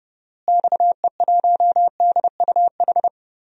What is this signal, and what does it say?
Morse code, heard as a keyed tone, about 30 wpm, 700 Hz. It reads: XE1DUH
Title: DXCC-100 Call Signs - 6 Characters - Repeated 2x - 30wpm